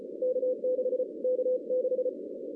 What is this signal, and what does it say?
Morse code, heard as a keyed tone, about 30 wpm, 500 Hz. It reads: K6 K6
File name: 2016 arrl dx cw / 2016-ARRL-DX-CW-K5TR- - 0-25